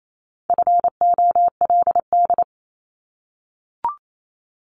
Morse code, heard as a keyed tone, about 28 wpm, 700 Hz. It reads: FOLD
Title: Multi-letter Phonemes Repeated 3x 28wpm